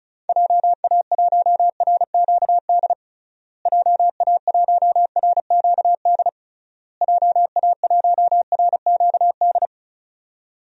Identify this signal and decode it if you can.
Morse code, heard as a keyed tone, about 36 wpm, 700 Hz. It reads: JA1RQD JA1RQD JA1RQD
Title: DXCC-100 Call Signs - 6 Characters - Repeated 3x - 35wpm